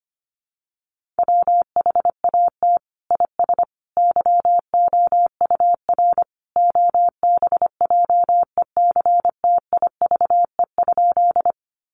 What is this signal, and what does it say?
Morse code, heard as a keyed tone, about 25 wpm, 700 Hz. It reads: WHAT IS YOUR OBJECTIVE?
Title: Sentences from Top 1200 Words 25wpm